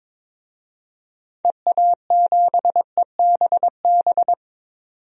Morse code, heard as a keyed tone, about 22 wpm, 700 Hz. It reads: EA7EBB
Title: DXCC-100 Call Signs - 6 Characters - 22wpm